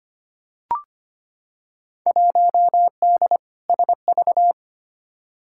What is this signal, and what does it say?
Morse code, heard as a keyed tone, about 25 wpm, 700 Hz. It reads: 1D SV
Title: Field Day Exchanges - 25wpm